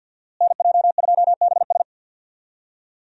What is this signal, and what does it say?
Morse code, heard as a keyed tone, about 47 wpm, 700 Hz. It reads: NJ2BS
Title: USA Call Signs - 5 Characters - Repeated 3x - 50wpm